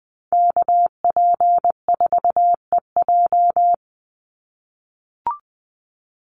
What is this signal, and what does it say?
Morse code, heard as a keyed tone, about 20 wpm, 700 Hz. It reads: KP4EJ
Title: Call Signs - 20wpm